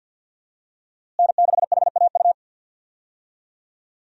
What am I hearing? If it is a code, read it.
Morse code, heard as a keyed tone, about 52 wpm, 700 Hz. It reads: N6HAU
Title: USA Call Signs - 5 Characters - Repeated 2x - 50wpm